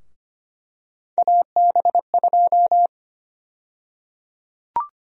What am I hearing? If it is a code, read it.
Morse code, heard as a keyed tone, about 25 wpm, 700 Hz. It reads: AB2
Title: USA Call Sign Prefixes - 25wpm